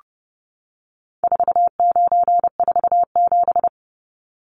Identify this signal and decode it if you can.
Morse code, heard as a keyed tone, about 30 wpm, 700 Hz. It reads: 4947
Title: Numbers - 4-Digit 30wpm